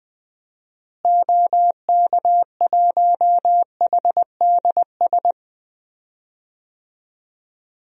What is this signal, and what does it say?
Morse code, heard as a keyed tone, about 20 wpm, 700 Hz. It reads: OK1HDS
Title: DXCC-100 Call Signs - 6 Characters - Repeated 3x - 20wpm